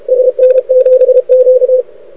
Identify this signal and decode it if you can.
Morse code, heard as a keyed tone, about 33 wpm, 500 Hz. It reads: SN7Q
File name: rec contest / arrl cw 2018 / test 32